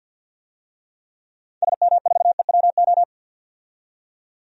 Morse code, heard as a keyed tone, about 49 wpm, 700 Hz. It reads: SM4EWO
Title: DXCC-100 Call Signs - 6 Characters - 50wpm